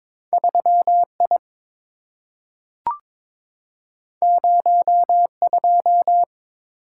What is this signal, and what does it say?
Morse code, heard as a keyed tone, about 22 wpm, 700 Hz. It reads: 3I 02
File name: practice / 22wpm / Two Letter-Number Combinations - 22wpm